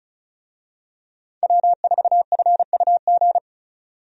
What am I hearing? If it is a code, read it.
Morse code, heard as a keyed tone, about 36 wpm, 700 Hz. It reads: W4FUG
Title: USA Call Signs - 5 Characters - 35wpm